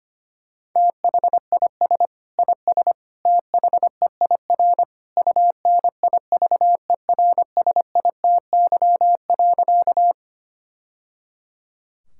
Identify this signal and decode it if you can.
Morse code, heard as a keyed tone, about 25 wpm, 700 Hz. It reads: THIS IS THEIR UNIVERSITY.
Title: Sentences from Top 1000 Words - Review Part 5 - 25wpm